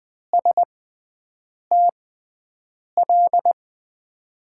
Morse code, heard as a keyed tone, about 20 wpm, spaced as for 7 wpm, 700 Hz. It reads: STL
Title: USA Call Signs - 5 Characters - 6x Character Spacing - 20wpm